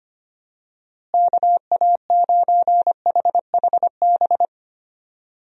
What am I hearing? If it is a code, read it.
Morse code, heard as a keyed tone, about 25 wpm, 700 Hz. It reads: KA9HHB
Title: USA Call Signs - 6 Characters - Repeated 3x - 25wpm